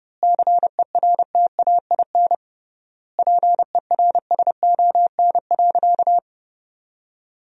Morse code, heard as a keyed tone, about 31 wpm, 700 Hz. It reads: CERTAIN PERSON.
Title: 3x Word-Spacing - Sentences from Top 500 Words - 30wpm